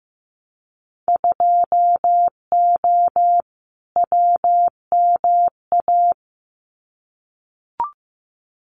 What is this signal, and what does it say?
Morse code, heard as a keyed tone, about 15 wpm, 700 Hz. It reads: 2O WMA
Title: Winter Field Day Exchanges Encore 15wpm